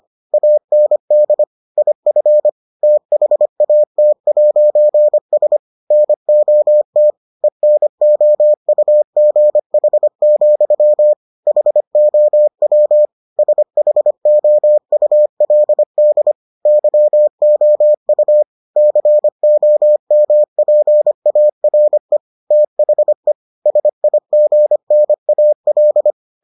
Morse code, heard as a keyed tone, about 25 wpm, 600 Hz. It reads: AND IF THAT'S NOT ENOUGH, HOW SHOULD YOU COMPARE THE SIGNAL